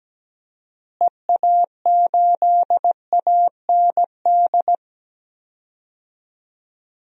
Morse code, heard as a keyed tone, about 17 wpm, 700 Hz. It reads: EA8AND
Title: DXCC-100 Call Signs - 6 Characters - Repeated 2x - 17wpm